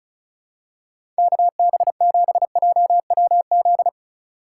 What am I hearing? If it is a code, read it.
Morse code, heard as a keyed tone, about 36 wpm, 700 Hz. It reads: KB7JWZ
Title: USA Call Signs - 6 Characters - 2x Character Spacing - 35wpm